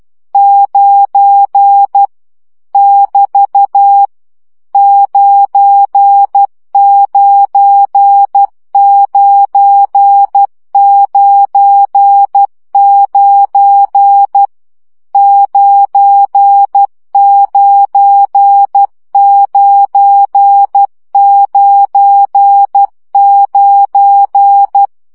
Morse code, heard as a keyed tone, about 12 wpm, 800 Hz. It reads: 9 <BT> 99999 99999
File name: CW gyakorlas 9 10 baud